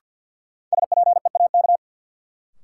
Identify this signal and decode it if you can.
Morse code, heard as a keyed tone, about 53 wpm, 700 Hz. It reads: SPEAK
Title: Sentences from Top 1000 Words - Review Part 2 - New Word Drill - 50wpm